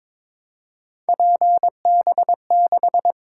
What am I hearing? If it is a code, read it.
Morse code, heard as a keyed tone, about 22 wpm, 700 Hz. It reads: PB6
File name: Three Letter-Number Combinations - 22wpm